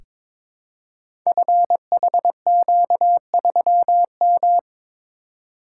Morse code, heard as a keyed tone, about 22 wpm, 700 Hz. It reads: FHQ3M
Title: Code Groups - 5 Letter:Numbers - 22wpm